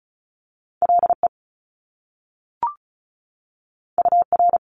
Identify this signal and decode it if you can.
Morse code, heard as a keyed tone, about 36 wpm, 700 Hz. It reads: LE UR
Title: Top Two Letter Combinations in Words 35wpm